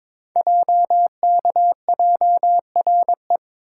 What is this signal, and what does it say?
Morse code, heard as a keyed tone, about 23 wpm, 700 Hz. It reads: JKJRE
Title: Code Groups - 5 Letters - 22wpm